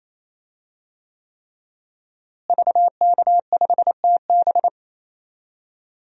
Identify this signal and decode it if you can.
Morse code, heard as a keyed tone, about 28 wpm, 700 Hz. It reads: VK5TB
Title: DXCC-100 Call Signs - 5 Characters - Repeated 3x - 28wpm